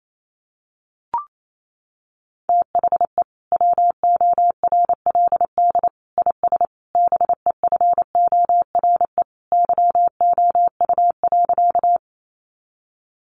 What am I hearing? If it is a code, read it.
Morse code, heard as a keyed tone, about 28 wpm, 700 Hz. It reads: THE WORLD IS BEFORE YOU.
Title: Sentences from Top 200 Words 28wpm